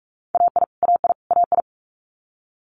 Morse code, heard as a keyed tone, about 46 wpm, 700 Hz. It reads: AI AI AI
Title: Multi-letter Phonemes Repeated 3x 45wpm